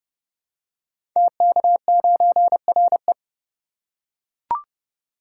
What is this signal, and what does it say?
Morse code, heard as a keyed tone, about 30 wpm, 700 Hz. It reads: TK9RE
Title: Code Groups - 5 Letter:Numbers - 30wpm